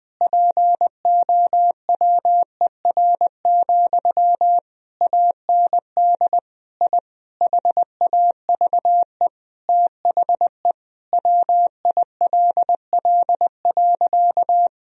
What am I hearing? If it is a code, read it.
Morse code, heard as a keyed tone, about 20 wpm, 700 Hz. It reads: POWER, AND I HAVE THE WILL.
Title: Sentences from Top 1000 Words - Review Part 2 - 20wpm